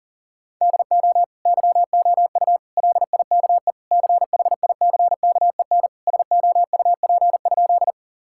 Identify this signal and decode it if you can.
Morse code, heard as a keyed tone, about 37 wpm, 700 Hz. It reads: DO YOU LIKE CHICKEN SOUP?
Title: Sentences from Top 1000 Words - Review Part 6 - 40wpm